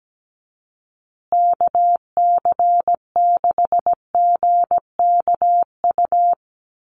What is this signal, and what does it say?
Morse code, heard as a keyed tone, about 17 wpm, 700 Hz. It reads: KC6GKU
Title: USA Call Signs Repeated 3x 17wpm